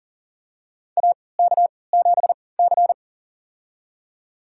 Morse code, heard as a keyed tone, about 40 wpm, spaced as for 23 wpm, 700 Hz. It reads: AK7C
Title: USA Call Signs - 4 Characters - 3x Character Spacing - 40wpm